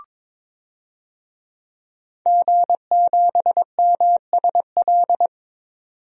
Morse code, heard as a keyed tone, about 21 wpm, 700 Hz. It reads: G7MSL
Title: DXCC-100 Call Signs - 5 Characters - 22wpm